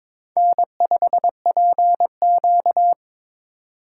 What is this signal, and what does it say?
Morse code, heard as a keyed tone, about 22 wpm, 700 Hz. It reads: N5PQ
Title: USA Call Signs - 4 Characters - 3x Character Spacing - 22wpm